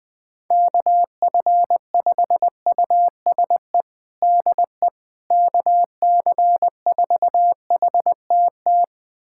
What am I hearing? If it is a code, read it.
Morse code, heard as a keyed tone, about 21 wpm, 700 Hz. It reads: KF5USE DE KC4HTT
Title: USA Callsign DE Callsign 20wpm